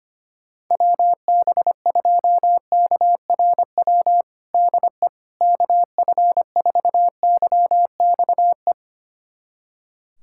Morse code, heard as a keyed tone, about 25 wpm, 700 Hz. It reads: WB2KRW DE KF4YXE
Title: USA Callsign DE Callsign - 6 Characters - 25wpm